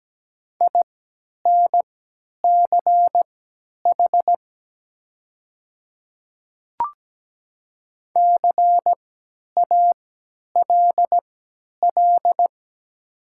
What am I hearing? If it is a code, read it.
Morse code, heard as a keyed tone, about 17 wpm, spaced as for 10 wpm, 700 Hz. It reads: INCH CALL
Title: 3x Character Spacing - Top 500 Words - 17wpm